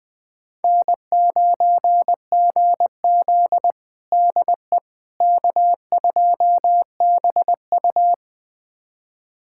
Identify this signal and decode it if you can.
Morse code, heard as a keyed tone, about 20 wpm, 700 Hz. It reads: N9GZ DE K2BU
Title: USA Callsign DE Callsign - 4 Characters - 20wpm